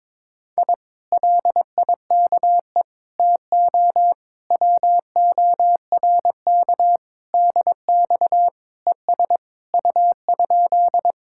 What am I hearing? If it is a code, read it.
Morse code, heard as a keyed tone, about 22 wpm, 700 Hz. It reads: I LIKE TO WORK DX ES U?
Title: Speed-Racing Common QSO Phrases - 22wpm